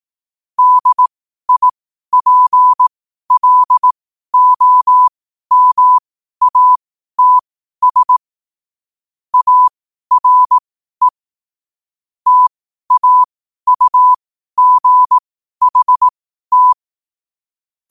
Morse code, heard as a keyed tone, about 18 wpm, spaced as for 12 wpm, 1000 Hz. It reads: DIPLOMATS ARE TAUGHT
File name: qotdfiles / 12wpm